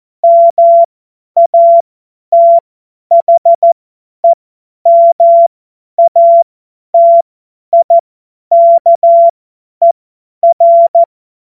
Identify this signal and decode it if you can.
Morse code, heard as a keyed tone, about 14 wpm, spaced as for 10 wpm, 680 Hz. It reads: MATHEMATIKER
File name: Morsekurs / Witze / Witz 21. Juni 2022 50 F70